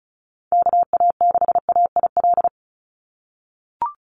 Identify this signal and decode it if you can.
Morse code, heard as a keyed tone, about 36 wpm, 700 Hz. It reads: KA6AIL
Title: Call Signs - 35wpm